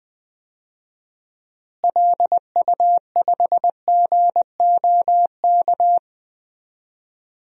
Morse code, heard as a keyed tone, about 20 wpm, 700 Hz. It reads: LU5GOK
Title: DXCC-100 Call Signs - 6 Characters - Repeated 3x - 20wpm